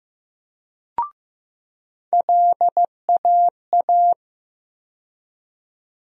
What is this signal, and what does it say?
Morse code, heard as a keyed tone, about 15 wpm, 700 Hz. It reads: LAA
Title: USA Call Sign Postfixes - 3 Letters 15wpm